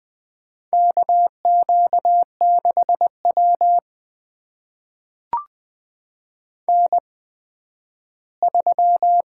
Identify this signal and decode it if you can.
Morse code, heard as a keyed tone, about 20 wpm, spaced as for 12 wpm, 700 Hz. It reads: KQ6W N3
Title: USA Call Signs - 4 Characters - 8x Character Spacing - 20wpm